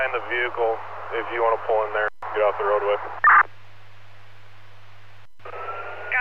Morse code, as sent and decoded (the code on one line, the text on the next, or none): none